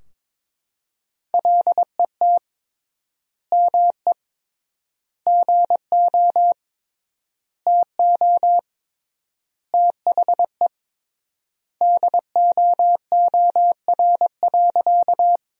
Morse code, heard as a keyed tone, about 22 wpm, 700 Hz. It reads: LET ME GO TO THE DOOR.
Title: 3x Word-Spacing - Sentences from Top 300 Words - 22wpm